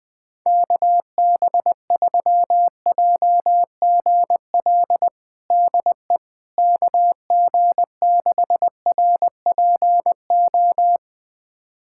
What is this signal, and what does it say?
Morse code, heard as a keyed tone, about 21 wpm, 700 Hz. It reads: KB3JGL DE KG6RPO
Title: USA Callsign DE Callsign 20wpm